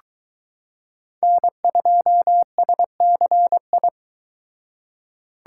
Morse code, heard as a keyed tone, about 22 wpm, 700 Hz. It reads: N2SCI